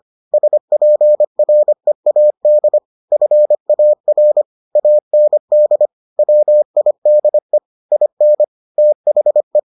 Morse code, heard as a keyed tone, about 25 wpm, 600 Hz. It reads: SPREAD FAR AND WIDE IN THE